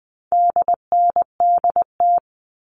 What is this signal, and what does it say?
Morse code, heard as a keyed tone, about 20 wpm, 700 Hz. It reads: DNDT
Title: All Cut Numbers - 4-Digit 20wpm